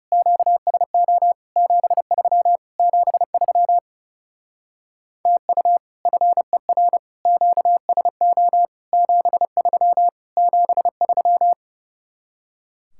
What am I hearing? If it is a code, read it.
Morse code, heard as a keyed tone, about 38 wpm, 700 Hz. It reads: QSO 73 73 TU FER QSO 73 73